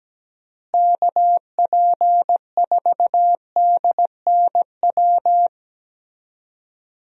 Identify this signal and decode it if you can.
Morse code, heard as a keyed tone, about 17 wpm, 700 Hz. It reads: KP4DNW
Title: USA Call Signs - 6 Characters - Repeated 3x - 17wpm